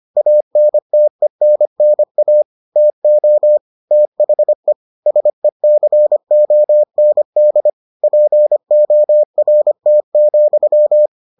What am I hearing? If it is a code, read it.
Morse code, heard as a keyed tone, about 25 wpm, 600 Hz. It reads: ANTENNA TO THE SECOND PORT,